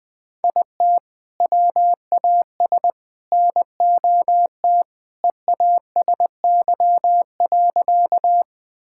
Morse code, heard as a keyed tone, about 20 wpm, 700 Hz. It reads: IT WAS NOT EASY.
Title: Sentences with 3-Word Phrases - 20wpm